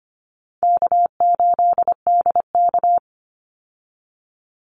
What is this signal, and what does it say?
Morse code, heard as a keyed tone, about 25 wpm, 700 Hz. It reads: K8DK
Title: Call Signs - 25wpm